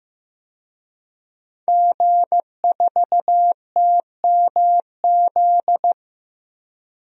Morse code, heard as a keyed tone, about 15 wpm, 700 Hz. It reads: G4TMZ
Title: DXCC-100 Call Signs - 5 Characters - 15wpm